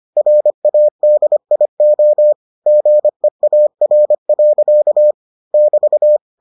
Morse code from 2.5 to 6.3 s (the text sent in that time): GEAR. <BT>